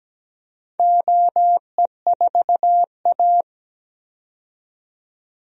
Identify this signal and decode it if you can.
Morse code, heard as a keyed tone, about 17 wpm, 700 Hz. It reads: OE4A
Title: DXCC-100 Call Signs - 4 Characters - 17wpm